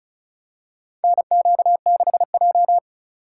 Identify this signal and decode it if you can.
Morse code, heard as a keyed tone, about 34 wpm, 700 Hz. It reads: NQ6J
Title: USA Call Signs - 4 Characters - 8x Character Spacing - 35wpm